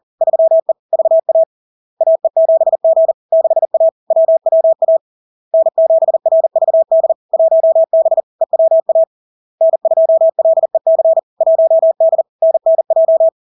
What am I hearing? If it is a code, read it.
Morse code, heard as a keyed tone, about 40 wpm, 650 Hz. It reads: 3E VA AE7G 6A WWA N7RVD 1B EWA N2LEC 1D NNJ
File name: LARK-FD-2022-Log-in-CW-40wpm-650-Hz